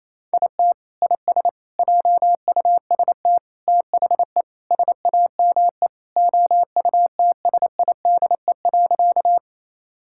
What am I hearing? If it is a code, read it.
Morse code, heard as a keyed tone, about 27 wpm, 700 Hz. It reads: IT IS JUST THE SAME OUTSIDE.
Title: Sentences from Top 1000 Words - Review Part 6 - 28wpm